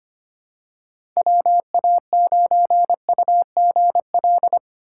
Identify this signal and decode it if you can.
Morse code, heard as a keyed tone, about 25 wpm, 700 Hz. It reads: WA9UGL
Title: USA Call Signs - 6 Characters - 3x Character Spacing - 25wpm